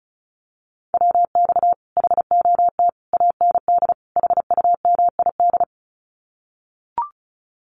Morse code, heard as a keyed tone, about 36 wpm, 700 Hz. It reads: WX HOT AND HUMID
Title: Common QSO Phrases Encore 35wpm